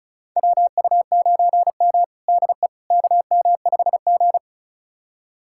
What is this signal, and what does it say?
Morse code, heard as a keyed tone, about 36 wpm, 700 Hz. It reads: WU9M DE KM5G
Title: USA Callsign DE Callsign - 4 Characters - 35wpm